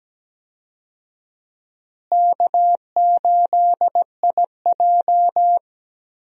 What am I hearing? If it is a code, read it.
Morse code, heard as a keyed tone, about 17 wpm, 700 Hz. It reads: K8IJ
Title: USA Call Signs - 4 Characters - Repeated 3x - 17wpm